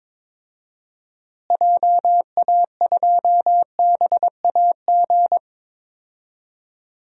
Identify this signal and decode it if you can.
Morse code, heard as a keyed tone, about 22 wpm, 700 Hz. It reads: JA2BAG
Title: DXCC-100 Call Signs - 6 Characters - Repeated 3x - 22wpm